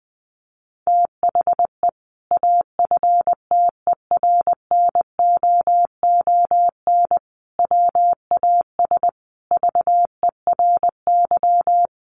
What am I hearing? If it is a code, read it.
Morse code, heard as a keyed tone, about 20 wpm, 700 Hz. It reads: THE AFTERNOON WAS VERY
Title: Sentences from Top 1500 Words 20wpm